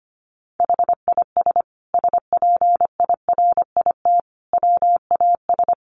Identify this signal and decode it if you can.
Morse code, heard as a keyed tone, about 25 wpm, 700 Hz. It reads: HIS SPIRIT WAS
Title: Sentences from Top 1400 Words 25wpm